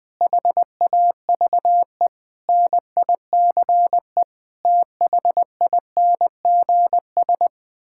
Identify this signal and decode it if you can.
Morse code, heard as a keyed tone, about 21 wpm, 700 Hz. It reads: HAVE NICE THINGS
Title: Sets of 8 Words - 20wpm